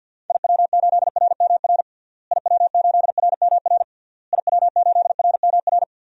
Morse code, heard as a keyed tone, about 51 wpm, 700 Hz. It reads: IW8RMR IW8RMR IW8RMR
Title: DXCC-100 Call Signs - 6 Characters - Repeated 3x - 50wpm